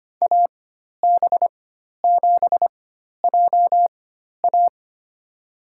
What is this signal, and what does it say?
Morse code, heard as a keyed tone, about 25 wpm, spaced as for 12 wpm, 700 Hz. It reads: AB7JA
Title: USA Call Signs - 5 Characters - 4x Character Spacing - 25wpm